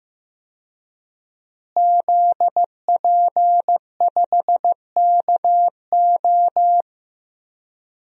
Morse code, heard as a keyed tone, about 15 wpm, 700 Hz. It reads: ZP5KO
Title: DXCC-100 Call Signs - 5 Characters - Repeated 3x - 15wpm